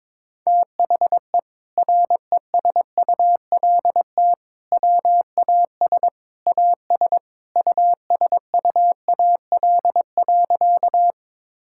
Morse code, heard as a keyed tone, about 22 wpm, 700 Hz. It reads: THE RESULT WAS AS USUAL.
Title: Sentences from Top 1000 Words - Review Part 5 - 22wpm